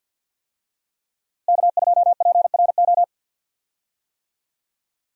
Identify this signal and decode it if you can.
Morse code, heard as a keyed tone, about 48 wpm, 700 Hz. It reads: K2PRO
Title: USA Call Signs - 5 Characters - Repeated 3x - 50wpm